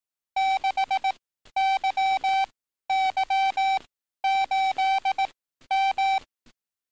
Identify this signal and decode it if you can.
Morse code, heard as a keyed tone, about 18 wpm, spaced as for 12 wpm, 770 Hz. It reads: 6YY8M
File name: training / NAVPERS 11049to11060 / NAVPERS 11052 RA RB